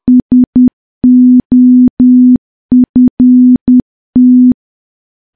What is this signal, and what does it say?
Morse code, heard as a keyed tone, about 10 wpm, 250 Hz. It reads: SOFT
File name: morse / morse-training-10WPM